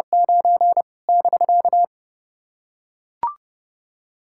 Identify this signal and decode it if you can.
Morse code, heard as a keyed tone, about 30 wpm, 700 Hz. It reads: 9 <BK>
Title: SOTA QSO Practice 30wpm